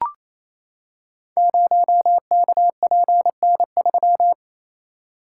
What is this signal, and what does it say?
Morse code, heard as a keyed tone, about 29 wpm, 700 Hz. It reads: 0KPN3
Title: Code Groups - 5 Letter:Numbers - 28wpm